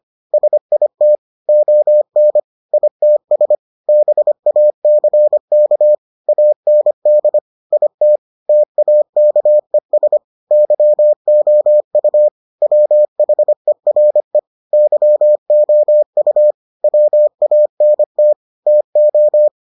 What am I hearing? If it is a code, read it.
Morse code, heard as a keyed tone, about 25 wpm, 600 Hz. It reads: SIT ON ITS BACK AND IT TAKES YOU WHERE YOU WANT TO